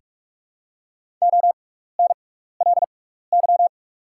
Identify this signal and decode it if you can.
Morse code, heard as a keyed tone, about 48 wpm, spaced as for 16 wpm, 700 Hz. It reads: ONLY
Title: 6x Character Spacing - Top 200 Words - 45wpm